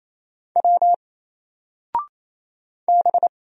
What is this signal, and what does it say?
Morse code, heard as a keyed tone, about 28 wpm, 700 Hz. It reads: W B
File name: Single-Letter-Number-28wpm